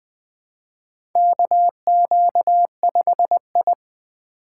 Morse code, heard as a keyed tone, about 20 wpm, 700 Hz. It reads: KQ5I
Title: USA Call Signs - 4 Characters - 3x Character Spacing - 20wpm